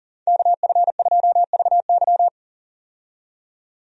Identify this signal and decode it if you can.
Morse code, heard as a keyed tone, about 40 wpm, 700 Hz. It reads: KF2VY
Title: USA Call Signs - 5 Characters - 40wpm